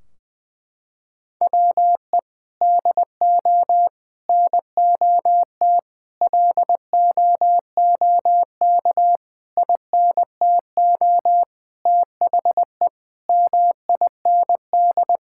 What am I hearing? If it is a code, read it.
Morse code, heard as a keyed tone, about 20 wpm, 700 Hz. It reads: WE DO NOT LOOK INTO THE MIND
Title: Sentences from Top 1600 Words 20wpm